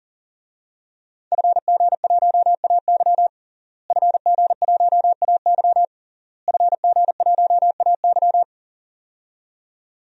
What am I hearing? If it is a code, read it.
Morse code, heard as a keyed tone, about 40 wpm, 700 Hz. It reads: FG1AY FG1AY FG1AY
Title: DXCC-100 Call Signs - 5 Characters - Repeated 3x - 40wpm